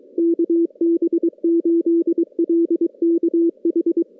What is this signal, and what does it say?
Morse code, heard as a keyed tone, about 22 wpm, 340 Hz. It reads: KB8LKH
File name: contests / CW-SS-k5tr- -2100